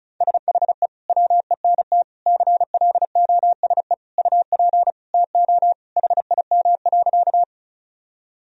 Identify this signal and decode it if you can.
Morse code, heard as a keyed tone, about 34 wpm, 700 Hz. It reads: SHE WENT CLOSE UP TO HIM.
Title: Sentences from Top 1000 Words - Review Part 1 - 35wpm